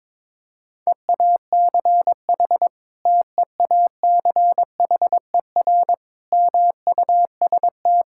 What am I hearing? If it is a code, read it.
Morse code, heard as a keyed tone, about 22 wpm, 700 Hz. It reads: EACH TEACHER MUST
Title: Sentences from Top 1000 Words - Review Part 6 - 22wpm